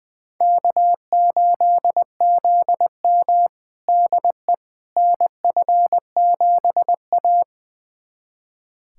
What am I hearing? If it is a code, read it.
Morse code, heard as a keyed tone, about 20 wpm, 700 Hz. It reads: K8ZM DE NF7A
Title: USA Callsign DE Callsign - 4 Characters - 20wpm